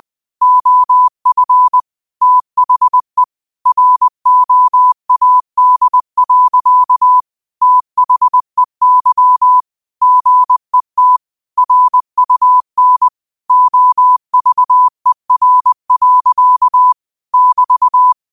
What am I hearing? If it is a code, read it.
Morse code, heard as a keyed tone, about 20 wpm, 1000 Hz. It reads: OF THE ROAD. THEY GET RUN OVER. <BT>